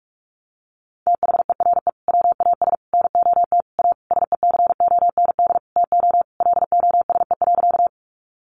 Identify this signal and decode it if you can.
Morse code, heard as a keyed tone, about 44 wpm, 700 Hz. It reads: THERE WAS NOT A SECOND TO LOSE.
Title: Sentences from Top 1100 Words 45wpm